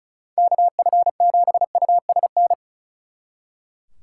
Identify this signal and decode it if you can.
Morse code, heard as a keyed tone, about 45 wpm, 700 Hz. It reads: KF7USN